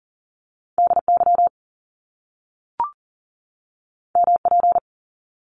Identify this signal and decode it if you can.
Morse code, heard as a keyed tone, about 39 wpm, 700 Hz. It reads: DY MP
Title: Top Two Letter Combinations in Words 40wpm